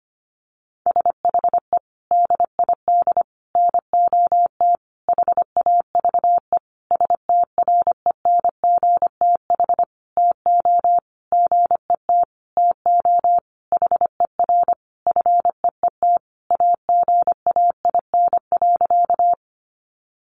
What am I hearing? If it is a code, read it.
Morse code, heard as a keyed tone, about 25 wpm, 700 Hz. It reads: SHE DID NOT HAVE STRENGTH TO GET TO HER FEET AGAIN.
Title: Sentences from Top 1300 Words 25wpm